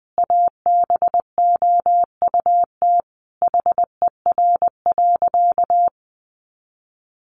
Morse code, heard as a keyed tone, about 20 wpm, 700 Hz. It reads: ABOUT HER.
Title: Sentences from Top 200 Words 20wpm